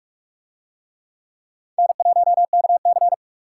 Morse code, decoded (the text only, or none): N1KC